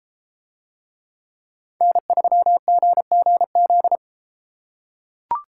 Morse code, heard as a keyed tone, about 32 wpm, 700 Hz. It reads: N3GGZ